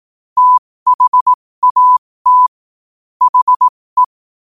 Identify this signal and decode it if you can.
Morse code, heard as a keyed tone, about 18 wpm, spaced as for 15 wpm, 1000 Hz. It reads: THAT HE
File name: qotdfiles / 15wpm